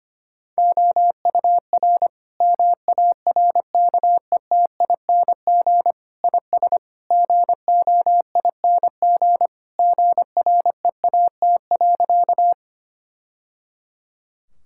Morse code, heard as a keyed tone, about 25 wpm, 700 Hz. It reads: OUR MARKETING IS GOING GREAT.
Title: Sentences from Top 1000 Words - Review Part 5 - 25wpm